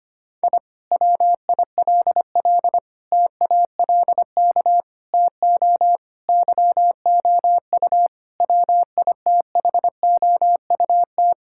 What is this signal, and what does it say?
Morse code, heard as a keyed tone, about 25 wpm, 700 Hz. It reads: I WILL TALK TO YOU WITHOUT
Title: Sentences from Top 1000 Words - Review Part 6 - 25wpm